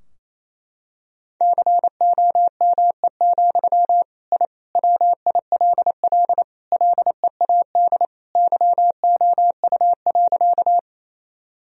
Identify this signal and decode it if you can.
Morse code, heard as a keyed tone, about 28 wpm, 700 Hz. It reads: COME, I WILL LEAD YOU.
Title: Sentences from Top 1000 Words - Review Part 2 - 28wpm